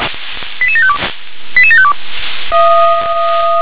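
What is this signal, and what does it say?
Five-tone selective call (ZVEI), heard as quick runs of five tones, tones of about 70 ms. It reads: 88752 88752
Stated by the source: ZVEI five-tone call received by SDR.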